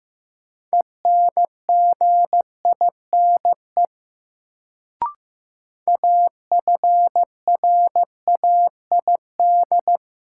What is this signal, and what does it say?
Morse code, heard as a keyed tone, about 15 wpm, 700 Hz. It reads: ENGINE AFRAID
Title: Sentences from Top 1500 Words - New Word Drill - 15wpm